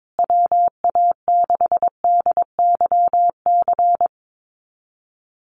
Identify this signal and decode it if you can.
Morse code, heard as a keyed tone, about 22 wpm, 700 Hz. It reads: WA6DYC
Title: USA Call Signs Repeated 3x 22wpm